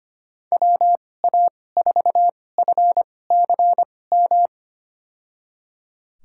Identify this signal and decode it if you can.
Morse code, heard as a keyed tone, about 25 wpm, spaced as for 18 wpm, 700 Hz. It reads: WA4FCM
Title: USA Call Signs - 2x Character Spacing - 25wpm